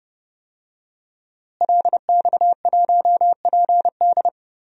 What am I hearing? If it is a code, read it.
Morse code, heard as a keyed tone, about 30 wpm, 700 Hz. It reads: LX1PD
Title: DXCC-100 Call Signs - 5 Characters - Repeated 3x - 30wpm